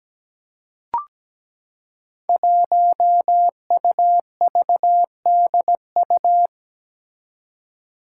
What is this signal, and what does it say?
Morse code, heard as a keyed tone, about 17 wpm, 700 Hz. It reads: 1UVDU
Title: Five Challenging Letter-Number Combinations 17wpm